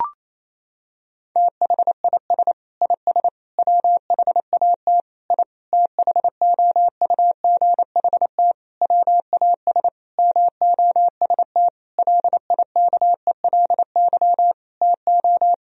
Morse code, heard as a keyed tone, about 28 wpm, 700 Hz. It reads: THIS IS WHAT I THOUGHT WAS MOST LIKELY TO
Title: Sentences from Top 1000 Words - Review Part 4 - 28wpm